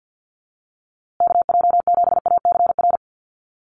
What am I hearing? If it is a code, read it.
Morse code, heard as a keyed tone, about 50 wpm, 700 Hz. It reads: KJ7ACR
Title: USA Call Signs Repeated 2x 50wpm